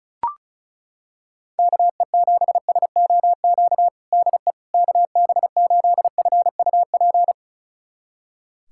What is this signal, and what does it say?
Morse code, heard as a keyed tone, about 34 wpm, 700 Hz. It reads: KE7SOQ DE KB8FUP
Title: USA Callsign DE Callsign 35wpm